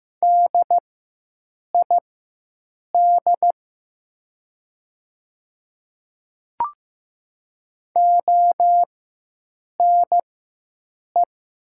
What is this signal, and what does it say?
Morse code, heard as a keyed tone, about 15 wpm, spaced as for 7 wpm, 700 Hz. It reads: DID ONE
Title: 4x Character Spacing - Top 100 Words - 15wpm